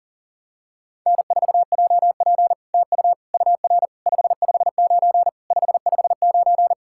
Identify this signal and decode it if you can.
Morse code, heard as a keyed tone, about 44 wpm, 700 Hz. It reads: N4JP TU UR 559 559
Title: SOTA QSO Practice 40wpm